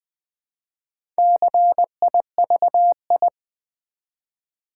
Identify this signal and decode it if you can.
Morse code, heard as a keyed tone, about 20 wpm, 700 Hz. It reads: CIVI
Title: Build A Word - 1 Letter At A Time - 5-Letter Words - 20wpm